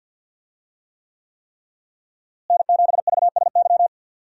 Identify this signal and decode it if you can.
Morse code, heard as a keyed tone, about 51 wpm, 700 Hz. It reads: N7VSY
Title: USA Call Signs - 5 Characters - Repeated 3x - 50wpm